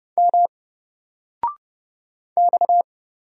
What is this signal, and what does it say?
Morse code, heard as a keyed tone, about 30 wpm, 700 Hz. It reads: M X
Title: Single Letter-Number 30wpm